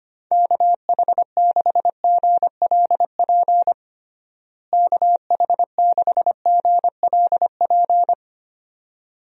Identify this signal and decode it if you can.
Morse code, heard as a keyed tone, about 25 wpm, 700 Hz. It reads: KH6GLP KH6GLP
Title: DXCC-100 Call Signs - 6 Characters - Repeated 2x - 25wpm